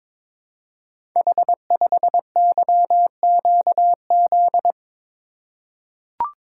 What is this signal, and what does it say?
Morse code, heard as a keyed tone, about 22 wpm, 700 Hz. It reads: H5YQZ
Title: Code Groups - 5 Letter:Numbers - 22wpm